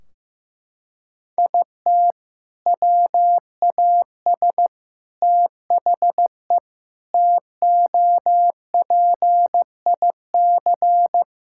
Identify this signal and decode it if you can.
Morse code, heard as a keyed tone, about 15 wpm, 700 Hz. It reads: IT WAS THE TOPIC
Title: Sentences from Top 1600 Words 15wpm